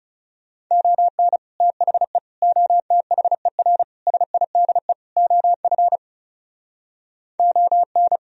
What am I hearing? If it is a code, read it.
Morse code, heard as a keyed tone, about 38 wpm, 700 Hz. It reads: ON THE OTHER SIDE OF ON